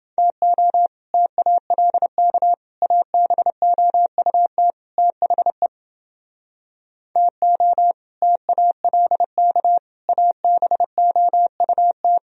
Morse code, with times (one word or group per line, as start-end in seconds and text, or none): 0.2-0.9 TO
1.1-2.5 TALK
2.8-4.7 ABOUT
5.0-5.7 THE
7.2-7.9 TO
8.2-9.8 TALK
10.1-12.2 ABOUT